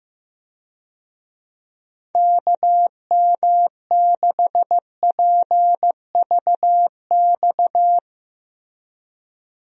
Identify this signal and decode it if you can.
Morse code, heard as a keyed tone, about 15 wpm, 700 Hz. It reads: KM6PVX
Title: USA Call Signs - 6 Characters - Repeated 3x - 15wpm